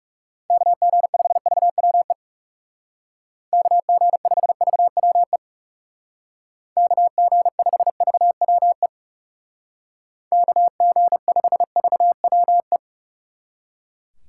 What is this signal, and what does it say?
Morse code, heard as a keyed tone, about 36 wpm, 700 Hz. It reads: KG5VWE KG5VWE KG5VWE KG5VWE
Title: Speed Racing - USA Call Signs 30wpm